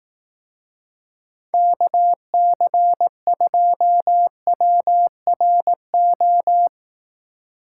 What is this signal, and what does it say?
Morse code, heard as a keyed tone, about 18 wpm, 700 Hz. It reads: KC2WRO
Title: Speed Racing - USA 2x3 Call Signs 15wpm